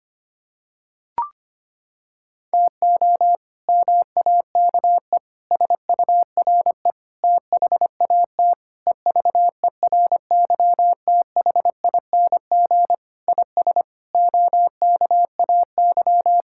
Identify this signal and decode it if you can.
Morse code, heard as a keyed tone, about 25 wpm, 700 Hz. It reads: TO MAKE SURE THAT EVERYTHING IS OKAY
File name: practice / 25wpm / Sentences with 4-Word Phrases - 25wpm